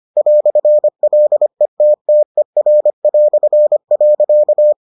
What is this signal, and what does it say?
Morse code, heard as a keyed tone, about 25 wpm, 600 Hz. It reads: "LETTER".